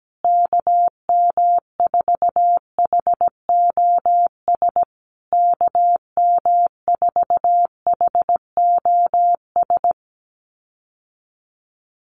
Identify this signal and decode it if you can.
Morse code, heard as a keyed tone, about 17 wpm, 700 Hz. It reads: KM4HOS KM4HOS
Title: USA Call Signs Repeated 3x 17wpm